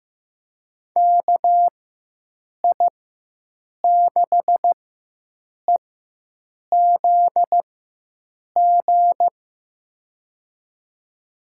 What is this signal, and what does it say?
Morse code, heard as a keyed tone, about 15 wpm, spaced as for 7 wpm, 700 Hz. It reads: KI6EZG
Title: USA Call Signs - 6 Characters - 4x Character Spacing - 15wpm